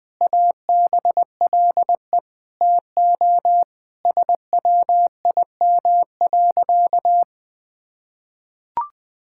Morse code, heard as a keyed tone, about 19 wpm, 700 Hz. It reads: ABLE TO SWIM.
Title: Sentences with 5-Word Phrases - 20wpm